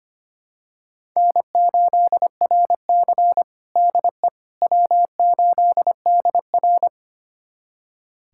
Morse code, heard as a keyed tone, about 25 wpm, 700 Hz. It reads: N8RC DE W8DR
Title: USA Callsign DE Callsign - 4 Characters - 25wpm